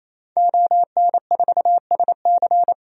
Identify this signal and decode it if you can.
Morse code, heard as a keyed tone, about 28 wpm, 700 Hz. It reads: ON4SC
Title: DXCC-100 Call Signs - 5 Characters - 28wpm